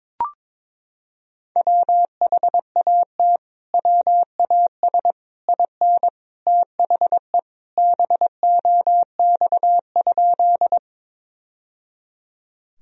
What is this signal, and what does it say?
Morse code, heard as a keyed tone, about 22 wpm, 700 Hz. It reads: WHAT WAS IN THE BOX?
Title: Sentences from Top 1000 Words - Review Part 2 - 22wpm